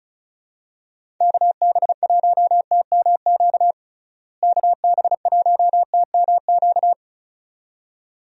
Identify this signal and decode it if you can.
Morse code, heard as a keyed tone, about 36 wpm, 700 Hz. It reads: KB1TMQ KB1TMQ
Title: USA Call Signs - 6 Characters - Repeated 2x - 35wpm